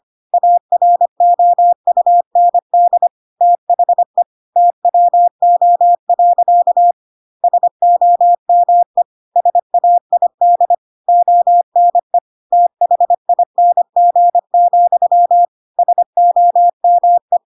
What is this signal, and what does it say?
Morse code, heard as a keyed tone, about 25 wpm, 700 Hz. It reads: AROUND THE TWO. SOME SAID ONE THING, SOME
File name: Pinocchio / Pinocchio ch3 25 25 ik1hge